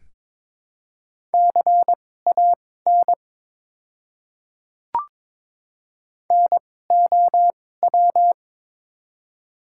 Morse code, heard as a keyed tone, about 22 wpm, spaced as for 16 wpm, 700 Hz. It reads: CAN NOW
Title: 2x Character Spacing - Top 100 Words - 22wpm